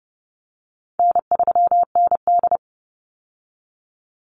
Morse code, heard as a keyed tone, about 30 wpm, 700 Hz. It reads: N3ND
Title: Easy Call Signs Repeated 3x - 30wpm